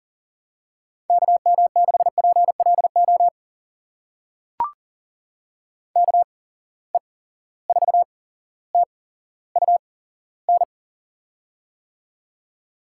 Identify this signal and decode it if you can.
Morse code, heard as a keyed tone, about 40 wpm, spaced as for 17 wpm, 700 Hz. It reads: KM6PLO KE4TUN
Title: USA Call Signs - 6 Characters - 8x Character Spacing - 40wpm